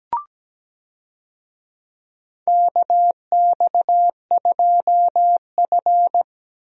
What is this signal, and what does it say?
Morse code, heard as a keyed tone, about 17 wpm, 700 Hz. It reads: KX2F
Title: USA Call Signs - 4 Characters - Repeated 2x - 17wpm